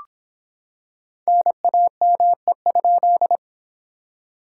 Morse code, heard as a keyed tone, about 26 wpm, 700 Hz. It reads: NAME?